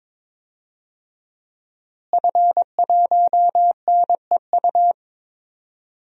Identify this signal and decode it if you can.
Morse code, heard as a keyed tone, about 22 wpm, 700 Hz. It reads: F1NEU
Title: DXCC-100 Call Signs - 5 Characters - Repeated 2x - 22wpm